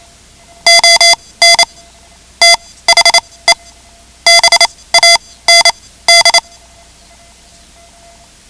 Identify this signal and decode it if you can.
Morse code, heard as a keyed tone, about 28 wpm, spaced as for 18 wpm, 2050 Hz. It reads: ON THE BAND
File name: ham radio / CodeQuick / Code Quick 10